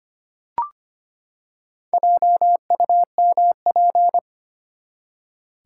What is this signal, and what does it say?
Morse code, heard as a keyed tone, about 25 wpm, 700 Hz. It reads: JUMP
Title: Four Letter Words - 25wpm